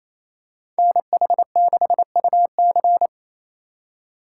Morse code, heard as a keyed tone, about 28 wpm, 700 Hz. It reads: NH6UC
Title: USA Call Signs - 5 Characters - 2x Character Spacing - 28wpm